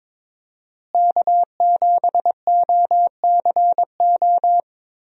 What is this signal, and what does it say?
Morse code, heard as a keyed tone, about 22 wpm, 700 Hz. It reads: K7OCO
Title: USA Call Signs - 5 Characters - Repeated 3x - 22wpm